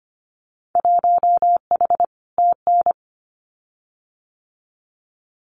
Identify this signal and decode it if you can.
Morse code, heard as a keyed tone, about 25 wpm, 700 Hz. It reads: 1H TN
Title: Winter Field Day Exchanges 25wpm